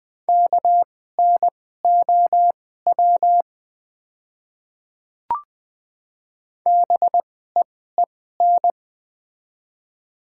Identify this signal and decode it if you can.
Morse code, heard as a keyed tone, about 21 wpm, spaced as for 15 wpm, 700 Hz. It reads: KNOW BEEN
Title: 2x Character Spacing - Top 200 Words - 20wpm